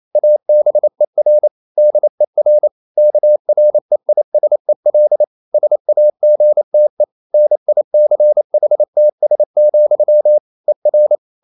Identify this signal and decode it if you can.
Morse code, heard as a keyed tone, about 28 wpm, 600 Hz. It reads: ABER DER KREISEL SAGTE NICHTS, ER
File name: Liebespaar4 28wpm